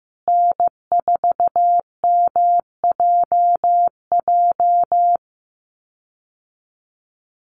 Morse code, heard as a keyed tone, about 15 wpm, 700 Hz. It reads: N4MJJ
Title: USA Call Signs Repeated 3x 15wpm